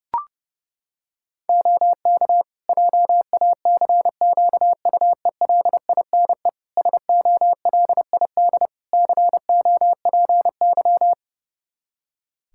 Morse code, heard as a keyed tone, about 30 wpm, 700 Hz. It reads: OK JACQUELINE SOLID COPY
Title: K1USN SST QSO Practice 30wpm